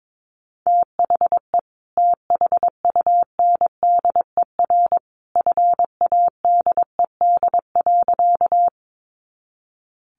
Morse code, heard as a keyed tone, about 22 wpm, 700 Hz. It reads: THE THUNDER FADED.
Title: A Sound Of Thunder 22wpm